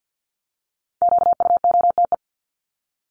Morse code, heard as a keyed tone, about 50 wpm, 700 Hz. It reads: QUOTE
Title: Top 1751-2000 Words 50wpm